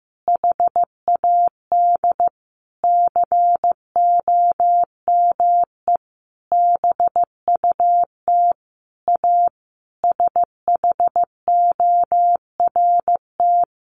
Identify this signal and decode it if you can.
Morse code, heard as a keyed tone, about 15 wpm, 700 Hz. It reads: HAD COME BUT A SHORT
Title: Sentences from Top 1400 Words 15wpm